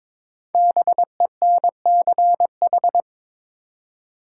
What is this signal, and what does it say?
Morse code, heard as a keyed tone, about 22 wpm, 700 Hz. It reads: BENCH
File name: 5 Letter Words - First 3 Letters Given - 22wpm